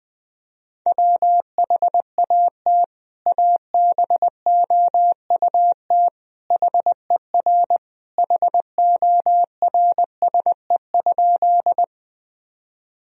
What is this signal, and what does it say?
Morse code, heard as a keyed tone, about 20 wpm, 700 Hz. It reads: WHAT ABOUT HER HORSE?
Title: Sentences from Top 1000 Words - Review Part 1 - 20wpm